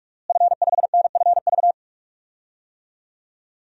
Morse code, heard as a keyed tone, about 45 wpm, 700 Hz. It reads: F5NFV
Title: DXCC-100 Call Signs - 5 Characters - Repeated 3x - 45wpm